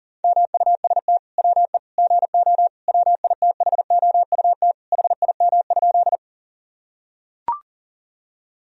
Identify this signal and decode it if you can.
Morse code, heard as a keyed tone, about 37 wpm, 700 Hz. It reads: MUST WE GO WITHOUT HIM?
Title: Sentences from Top 1000 Words - Review Part 3 - 40wpm